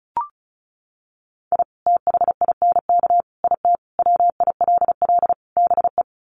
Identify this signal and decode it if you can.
Morse code, heard as a keyed tone, about 34 wpm, 700 Hz. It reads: I THINK IT WILL BE
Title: Sets of 5 Words Encore2 35wpm